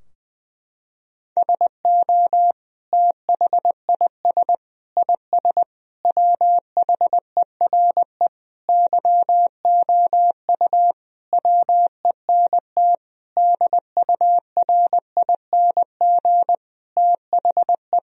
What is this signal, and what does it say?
Morse code, heard as a keyed tone, about 20 wpm, 700 Hz. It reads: SO THIS IS WHERE YOU WENT DURING THE
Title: Sentences from Top 1000 Words - Review Part 2 - 20wpm